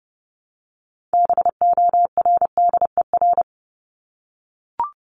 Morse code, heard as a keyed tone, about 30 wpm, 700 Hz. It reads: BORDER
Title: Top 1751-2000 Words 30wpm